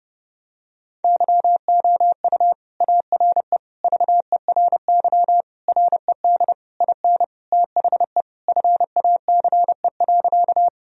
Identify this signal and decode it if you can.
Morse code, heard as a keyed tone, about 30 wpm, 700 Hz. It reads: YOU ARE VERY RED IN THE FACE.
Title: Sentences from Top 1000 Words - Review Part 2 - 30wpm